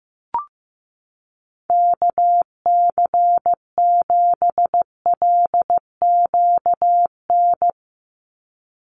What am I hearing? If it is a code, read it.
Morse code, heard as a keyed tone, about 15 wpm, 700 Hz. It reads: KC7LQN
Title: USA Call Signs 15wpm